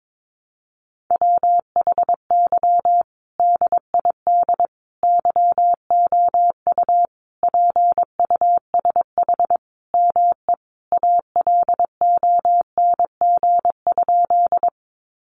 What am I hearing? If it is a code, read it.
Morse code, heard as a keyed tone, about 22 wpm, 700 Hz. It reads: WHY DID YOU PUSH ME ALONG?
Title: Sentences from Top 1300 Words 22wpm